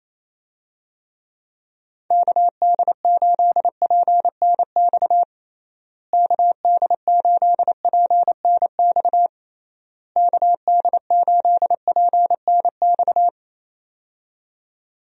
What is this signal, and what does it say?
Morse code, heard as a keyed tone, about 28 wpm, 700 Hz. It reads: KD8PNX KD8PNX KD8PNX
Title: USA Call Signs - 6 Characters - Repeated 3x - 28wpm